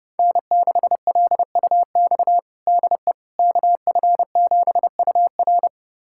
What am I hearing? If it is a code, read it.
Morse code, heard as a keyed tone, about 31 wpm, 700 Hz. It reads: N6LUX DE KF7UR
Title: USA Callsign DE Callsign - 5 Characters - 30wpm